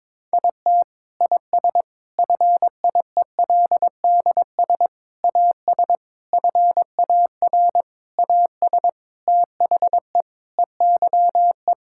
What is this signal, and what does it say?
Morse code, heard as a keyed tone, about 22 wpm, 700 Hz. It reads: IT IS FIELDS AS FAR AS THE EYE
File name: Sentences with 7-Word Phrases - 22wpm